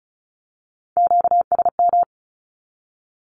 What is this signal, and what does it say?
Morse code, heard as a keyed tone, about 36 wpm, 700 Hz. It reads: QSM
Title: Q-Codes with Spoken Meaning - 35wpm